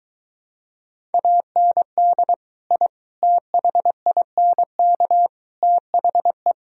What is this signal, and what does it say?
Morse code, heard as a keyed tone, about 22 wpm, 700 Hz. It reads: AND I THINK THE